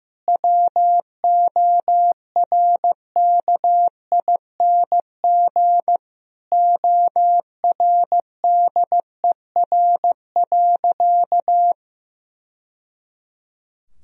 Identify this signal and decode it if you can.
Morse code, heard as a keyed tone, about 15 wpm, 700 Hz. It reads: WORKING ORDER.
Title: Sentences from Top 1000 Words - Review Part 4 - 15wpm